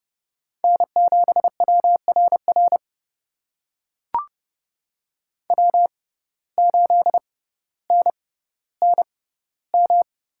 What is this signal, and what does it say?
Morse code, heard as a keyed tone, about 29 wpm, spaced as for 15 wpm, 700 Hz. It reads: N7WRR W8NNM
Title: USA Call Signs - 5 Characters - 6x Character Spacing - 30wpm